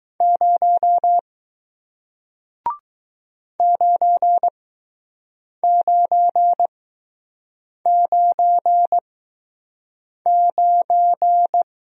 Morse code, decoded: 0 9999

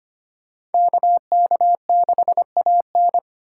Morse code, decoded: KK6AN

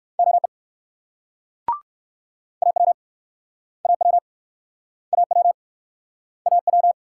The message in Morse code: OE AW AW AW AW

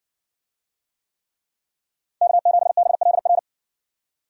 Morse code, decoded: K7BLR

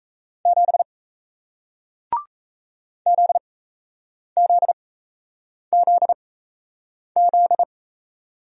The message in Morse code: 7 ZZZZ

